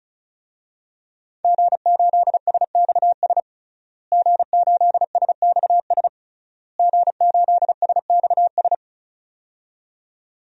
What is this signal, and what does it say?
Morse code, heard as a keyed tone, about 34 wpm, 700 Hz. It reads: G8SXS G8SXS G8SXS
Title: DXCC-100 Call Signs - 5 Characters - Repeated 3x - 35wpm